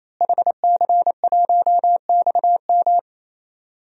Morse code, heard as a keyed tone, about 28 wpm, 700 Hz. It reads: HC1XM
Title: DXCC-100 Call Signs - 5 Characters - Repeated 3x - 28wpm